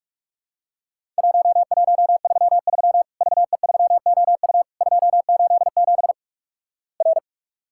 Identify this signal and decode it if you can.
Morse code, heard as a keyed tone, about 46 wpm, 700 Hz. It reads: 1133 VE3OU 287 R
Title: Worked All Europe DX Contest - QTC Exchange - 45wpm